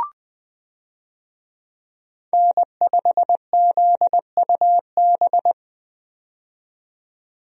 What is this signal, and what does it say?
Morse code, heard as a keyed tone, about 20 wpm, 700 Hz. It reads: N5ZUB
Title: USA Call Signs - 5 Characters - 20wpm